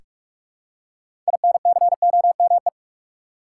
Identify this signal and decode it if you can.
Morse code, heard as a keyed tone, about 46 wpm, 700 Hz. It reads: INCOME
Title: Six Letter Words - 45wpm